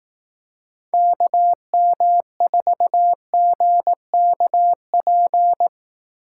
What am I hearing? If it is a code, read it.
Morse code, heard as a keyed tone, about 18 wpm, 700 Hz. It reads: KM4GKP
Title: Speed Racing - USA Call Signs 15wpm